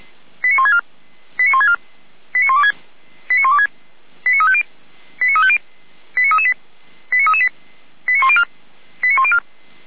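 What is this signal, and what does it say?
Five-tone selective call (ZVEI), heard as quick runs of five tones, tones of about 70 ms. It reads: 89164 89164 89126 89126 89350 89350 89338 89338 89104 89104
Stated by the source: ZVEI five-tone selective call received on an SDR.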